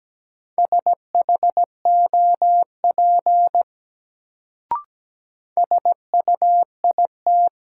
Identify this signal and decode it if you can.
Morse code, heard as a keyed tone, about 17 wpm, 700 Hz. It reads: SHOP SUIT
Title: Sentences from Top 1600 Words - New Word Drill Alt - 17wpm